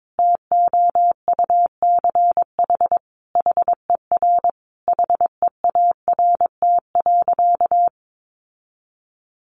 Morse code, heard as a keyed tone, about 22 wpm, 700 Hz. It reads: TOUCH HER HEART.
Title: Sentences from Top 1200 Words 22wpm